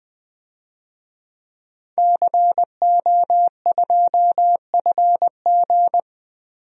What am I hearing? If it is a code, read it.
Morse code, heard as a keyed tone, about 20 wpm, 700 Hz. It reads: CO2FG
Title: DXCC-100 Call Signs - 5 Characters - Repeated 2x - 20wpm